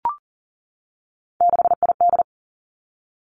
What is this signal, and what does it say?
Morse code, heard as a keyed tone, about 40 wpm, 700 Hz. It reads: 6ID
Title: Three Challenging Letter-Number Combinations 40wpm